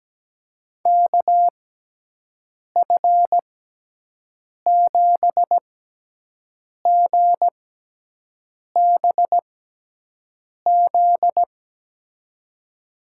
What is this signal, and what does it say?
Morse code, heard as a keyed tone, about 17 wpm, spaced as for 6 wpm, 700 Hz. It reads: KF7GBZ
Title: USA Call Signs - 6x Character Spacing - 17wpm